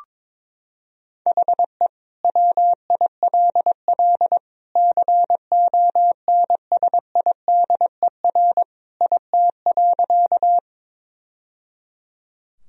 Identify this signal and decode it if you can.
Morse code, heard as a keyed tone, about 21 wpm, 700 Hz. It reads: HE WILL CONSIDER IT.
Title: Sentences from Top 1000 Words - Review Part 4 - 22wpm